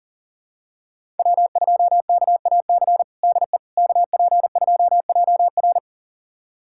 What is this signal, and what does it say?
Morse code, heard as a keyed tone, about 38 wpm, 700 Hz. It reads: W2KAC DE KP2JR
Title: USA Callsign DE Callsign - 5 Characters - 40wpm